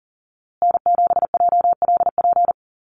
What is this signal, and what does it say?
Morse code, heard as a keyed tone, about 40 wpm, 700 Hz. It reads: N7JLP
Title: USA Call Signs Repeated 2x 40wpm